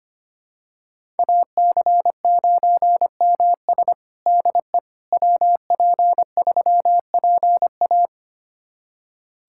Morse code, heard as a keyed tone, about 25 wpm, 700 Hz. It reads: AC9MS DE WP3PA
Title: USA Callsign DE Callsign - 5 Characters - 25wpm